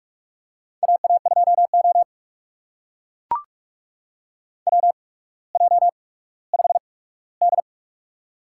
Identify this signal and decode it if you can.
Morse code, heard as a keyed tone, about 44 wpm, spaced as for 19 wpm, 700 Hz. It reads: AA2O WJ5D
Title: USA Call Signs - 8x Character Spacing - 45wpm